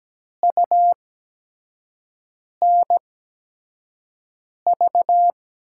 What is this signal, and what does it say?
Morse code, heard as a keyed tone, about 17 wpm, spaced as for 5 wpm, 700 Hz. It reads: UNV
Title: USA Call Signs - 6 Characters - 8x Character Spacing - 17wpm